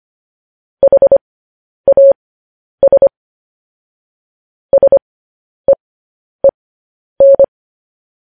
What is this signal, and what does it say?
Morse code, heard as a keyed tone, about 25 wpm, spaced as for 10 wpm, 550 Hz. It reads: HAS SEEN
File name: podcast / News headlines-1x-25x10